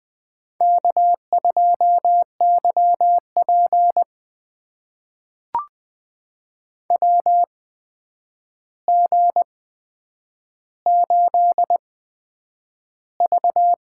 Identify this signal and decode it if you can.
Morse code, heard as a keyed tone, about 20 wpm, spaced as for 9 wpm, 700 Hz. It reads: K2YP WG8V
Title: USA Call Signs - 4 Characters - 8x Character Spacing - 20wpm